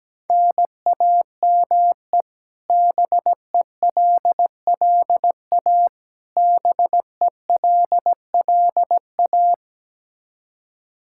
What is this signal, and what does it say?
Morse code, heard as a keyed tone, about 17 wpm, 700 Hz. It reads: NAME BELLA BELLA
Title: Practice QSO - Exchange 1 - Beginning - 17wpm